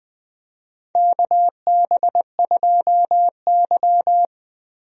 Morse code, heard as a keyed tone, about 20 wpm, 700 Hz. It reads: KB2Y
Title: USA Call Signs - 4 Characters - Repeated 3x - 20wpm